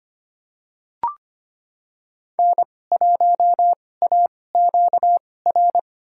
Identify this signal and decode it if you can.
Morse code, heard as a keyed tone, about 25 wpm, spaced as for 18 wpm, 700 Hz. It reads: N1AQR
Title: USA Call Signs - 5 Characters - 2x Character Spacing - 25wpm